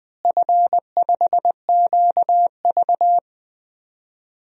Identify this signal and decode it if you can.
Morse code, heard as a keyed tone, about 21 wpm, 700 Hz. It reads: F5QV
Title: DXCC-100 Call Signs - 4 Characters - Repeated 3x - 20wpm